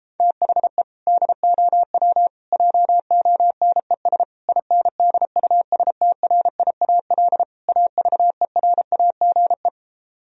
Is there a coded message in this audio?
Yes: Morse code from 0.2 to 9.7 s, heard as a keyed tone, about 33 wpm, 700 Hz. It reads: THE DOW JONES INDUSTRIAL AVERAGE